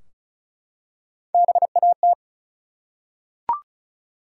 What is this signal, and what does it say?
Morse code, heard as a keyed tone, about 36 wpm, 700 Hz. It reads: BAT
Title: Three Letter Words - 35wpm